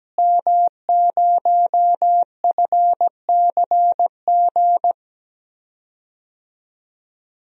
Morse code, heard as a keyed tone, about 17 wpm, 700 Hz. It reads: M0FCG
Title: DXCC-100 Call Signs - 5 Characters - Repeated 2x - 17wpm